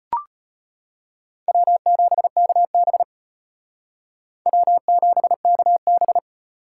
Morse code, heard as a keyed tone, about 33 wpm, 700 Hz. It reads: W7KB W7KB